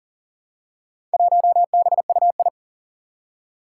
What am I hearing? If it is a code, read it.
Morse code, heard as a keyed tone, about 40 wpm, 700 Hz. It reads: 1BUI
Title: Four Challenging Letter-Number Combinations 40wpm